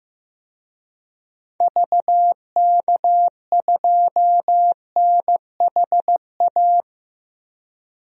Morse code, heard as a keyed tone, about 15 wpm, 700 Hz. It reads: VK2NHA
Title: DXCC-100 Call Signs - 6 Characters - 15wpm